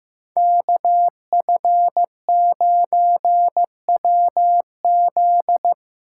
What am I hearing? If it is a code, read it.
Morse code, heard as a keyed tone, about 15 wpm, 700 Hz. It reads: KF9WZ
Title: USA Call Signs - 5 Characters - 6x Character Spacing - 15wpm